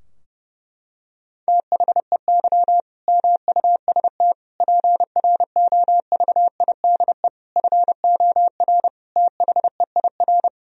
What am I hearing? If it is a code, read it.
Morse code, heard as a keyed tone, about 32 wpm, 700 Hz. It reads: THEY MUST PROVIDE FOR THEIR
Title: Sentences from Top 1000 Words - Review Part 3 - 30wpm